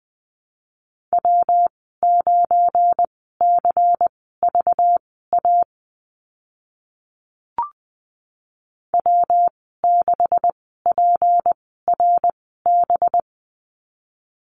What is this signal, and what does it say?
Morse code, heard as a keyed tone, about 20 wpm, spaced as for 14 wpm, 700 Hz. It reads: W9CVA W6PRB
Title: 193 Lesson 31g - Letter v - Call Signs 20wpm